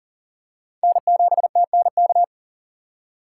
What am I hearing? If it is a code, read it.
Morse code, heard as a keyed tone, about 40 wpm, 700 Hz. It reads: N7TNK
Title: USA Call Signs - 6x Character Spacing - 40wpm